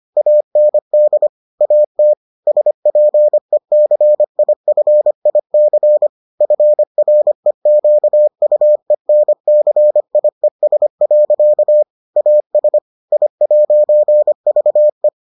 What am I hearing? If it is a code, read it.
Morse code, heard as a keyed tone, about 25 wpm, 600 Hz. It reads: AND AT SPECIFIC FREQUENCIES. AS I'VE